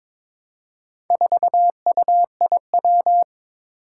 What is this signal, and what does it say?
Morse code, heard as a keyed tone, about 23 wpm, 700 Hz. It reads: 4UIW
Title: Four Challenging Letter-Number Combinations 22wpm